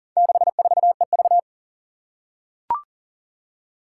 Morse code, heard as a keyed tone, about 40 wpm, 700 Hz. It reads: 64EV
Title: Four Challenging Letter-Number Combinations 40wpm